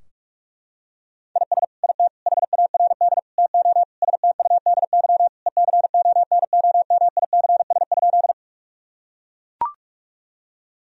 Morse code, heard as a keyed tone, about 43 wpm, 700 Hz. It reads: IS IT HARD TO STUDY ECONOMICS?
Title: Sentences from Top 1000 Words - Review Part 3 - 45wpm